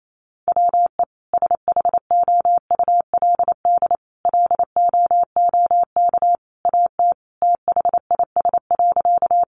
Morse code, heard as a keyed tone, about 28 wpm, 700 Hz. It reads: WE SHOULD LOOK AT THIS.
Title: Sentences from Top 300 Words 28wpm